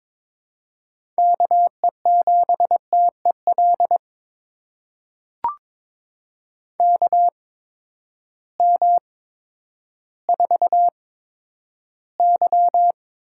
Morse code, heard as a keyed tone, about 22 wpm, 700 Hz. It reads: K E 7 T E L K M 4 Y
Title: USA Call Signs - 6 Characters - 8x Character Spacing - 22wpm